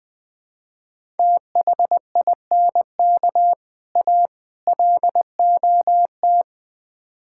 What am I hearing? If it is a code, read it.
Morse code, heard as a keyed tone, about 20 wpm, 700 Hz. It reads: THINK A LOT
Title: Build A Phrase - 5-Word Phrases - 20wpm